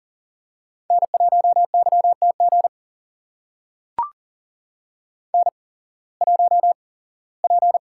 Morse code, heard as a keyed tone, about 42 wpm, spaced as for 22 wpm, 700 Hz. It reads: N1YTG N1P
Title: USA Call Signs - 5 Characters - 8x Character Spacing - 40wpm